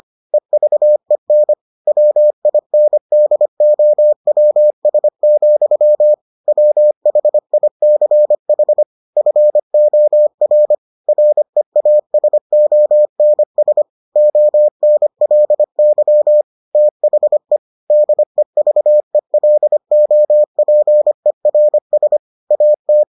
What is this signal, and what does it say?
Morse code, heard as a keyed tone, about 25 wpm, 600 Hz. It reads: EVEN WINDOWS, WHICH FOR REASONS ONLY THE DEVELOPERS AT